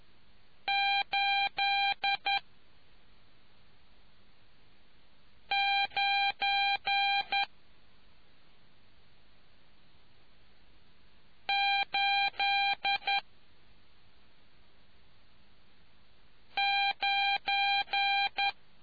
Morse code, heard as a keyed tone, about 11 wpm, spaced as for 2 wpm, 800 Hz. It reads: 89 89